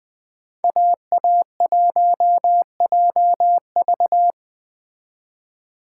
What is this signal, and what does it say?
Morse code, heard as a keyed tone, about 20 wpm, 700 Hz. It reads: AA1JV